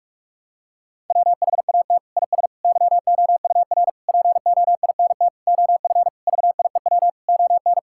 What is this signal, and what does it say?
Morse code, heard as a keyed tone, about 42 wpm, 700 Hz. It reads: WHAT IS YOUR POINT OF VIEW ON
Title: Sentences from Top 1000 Words - Review Part 6 - 45wpm